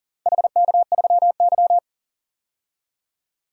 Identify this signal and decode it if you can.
Morse code, heard as a keyed tone, about 40 wpm, 700 Hz. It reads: HK3Y
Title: DXCC-100 Call Signs - 4 Characters - 40wpm